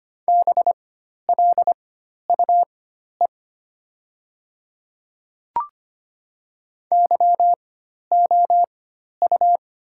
Morse code, heard as a keyed tone, about 25 wpm, spaced as for 12 wpm, 700 Hz. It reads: BLUE YOU
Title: 4x Character Spacing - Top 500 Words - 25wpm